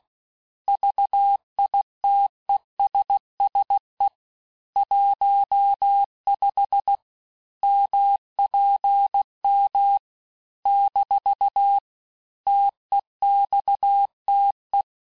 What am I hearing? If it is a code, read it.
Morse code, heard as a keyed tone, about 16 wpm, 800 Hz. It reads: VITESSE 15 MPM - TEXTE
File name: cours-du-lundi files / Cours-1586-texte-chiffre-vitesse-15mpm